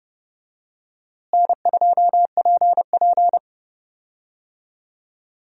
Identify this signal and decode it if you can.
Morse code, heard as a keyed tone, about 30 wpm, 700 Hz. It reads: N2PP
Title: USA Call Signs - 4 Characters - Repeated 2x - 30wpm